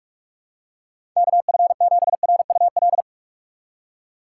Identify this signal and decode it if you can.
Morse code, heard as a keyed tone, about 41 wpm, 700 Hz. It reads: KF7RUL
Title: USA Call Signs - 6 Characters - 4x Character Spacing - 45wpm